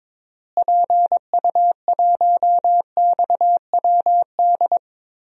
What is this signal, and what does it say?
Morse code, heard as a keyed tone, about 22 wpm, 700 Hz. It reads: PU1XWD
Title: DXCC-100 Call Signs - 6 Characters - Repeated 2x - 22wpm